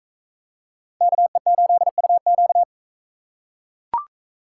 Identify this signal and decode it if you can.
Morse code, heard as a keyed tone, about 42 wpm, 700 Hz. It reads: KE8UQ